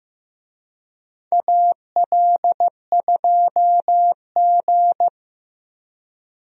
Morse code, heard as a keyed tone, about 15 wpm, 700 Hz. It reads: AL2G
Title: USA Call Signs - 4 Characters - 15wpm